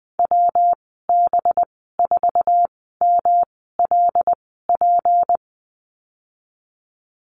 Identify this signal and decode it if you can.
Morse code, heard as a keyed tone, about 20 wpm, spaced as for 14 wpm, 700 Hz. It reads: WB4MLP
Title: 193 Lesson 31g - Letter v - Call Signs 20wpm